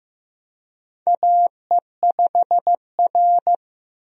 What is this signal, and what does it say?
Morse code, heard as a keyed tone, about 15 wpm, 700 Hz. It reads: AE5R
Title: USA Call Signs - 4 Characters - 15wpm